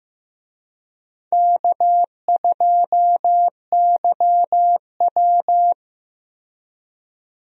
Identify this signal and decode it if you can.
Morse code, heard as a keyed tone, about 15 wpm, 700 Hz. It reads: K2YW
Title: USA Call Signs - 4 Characters - 15wpm